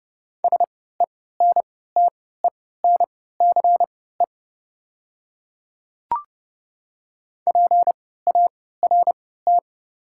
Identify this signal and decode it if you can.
Morse code, heard as a keyed tone, about 31 wpm, spaced as for 17 wpm, 690 Hz. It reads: SENTENCE PART
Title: 3x Character Spacing - Top 200 Words - 30wpm